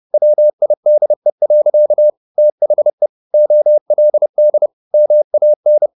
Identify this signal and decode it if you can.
Morse code, heard as a keyed tone, about 30 wpm, 600 Hz. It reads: WIDE. THE OLD MAN